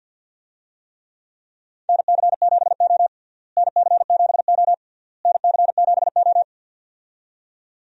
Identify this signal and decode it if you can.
Morse code, heard as a keyed tone, about 46 wpm, 700 Hz. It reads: NC7O NC7O NC7O
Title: USA Call Signs - 4 Characters - Repeated 3x - 50wpm